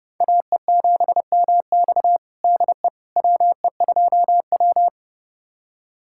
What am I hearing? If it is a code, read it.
Morse code, heard as a keyed tone, about 30 wpm, 700 Hz. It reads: AE7MX DE WE2W
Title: USA Callsign DE Callsign 30wpm